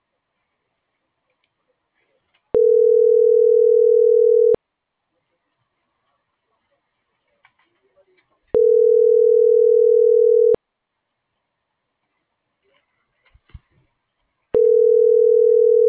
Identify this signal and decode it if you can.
Ringback tone (North America), 440 + 480 Hz.